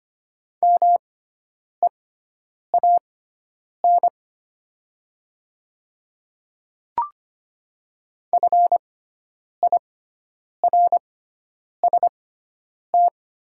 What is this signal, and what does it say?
Morse code, heard as a keyed tone, about 25 wpm, spaced as for 9 wpm, 700 Hz. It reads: MEAN FIRST
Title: 6x Character Spacing - Top 200 Words - 25wpm